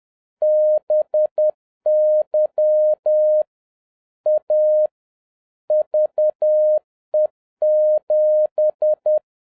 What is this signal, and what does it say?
Morse code, heard as a keyed tone, about 10 wpm, 610 Hz. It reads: BY A VE7